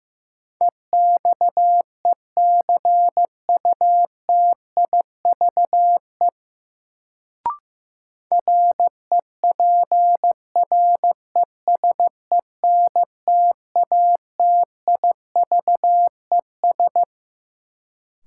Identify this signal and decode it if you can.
Morse code, heard as a keyed tone, about 15 wpm, 700 Hz. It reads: EXECUTIVE REPRESENTATIVES
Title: Sets of 5 Words Encore - New Word Drill - 15wpm